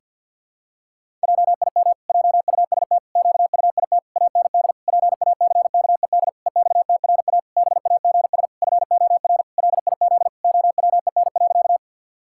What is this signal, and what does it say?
Morse code, heard as a keyed tone, about 51 wpm, 700 Hz. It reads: JIM JUST QUIT AND PACKED EXTRA BAGS FOR LIZ OWEN.
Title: Pangram Challenge v2 - 50wpm